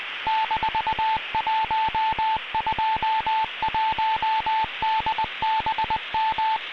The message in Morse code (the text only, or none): -121DBM